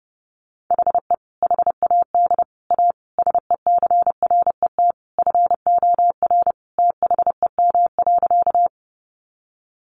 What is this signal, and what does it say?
Morse code, heard as a keyed tone, about 30 wpm, 700 Hz. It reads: HE HAD A SECRET FOR THEM.
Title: Sentences from Top 1200 Words 30wpm